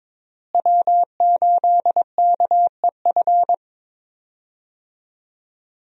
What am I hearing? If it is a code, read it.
Morse code, heard as a keyed tone, about 22 wpm, 700 Hz. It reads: W8KEF
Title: USA Call Signs - 5 Characters - Repeated 2x - 22wpm